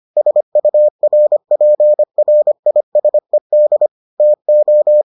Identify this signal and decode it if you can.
Morse code, heard as a keyed tone, about 25 wpm, 600 Hz. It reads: SURPRISED TO